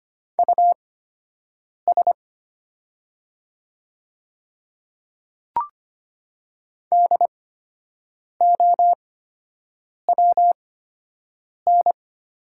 Morse code, heard as a keyed tone, about 25 wpm, spaced as for 7 wpm, 690 Hz. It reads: US DOWN
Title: 8x Character Spacing - Top 200 Words - 25wpm